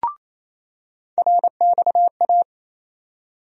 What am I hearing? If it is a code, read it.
Morse code, heard as a keyed tone, about 28 wpm, 700 Hz. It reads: RXA
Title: USA Call Sign Postfixes - 3 Letters 28wpm